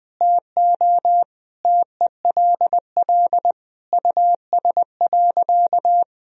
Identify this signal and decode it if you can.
Morse code, heard as a keyed tone, about 20 wpm, 700 Hz. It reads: TO TELL US.
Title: Sentences from Top 1000 Words - Review Part 3 - 20wpm